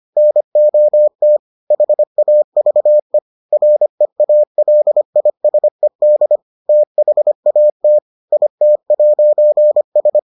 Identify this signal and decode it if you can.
Morse code, heard as a keyed tone, about 25 wpm, 600 Hz. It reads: NOT HAVE REALISED THAT IT'S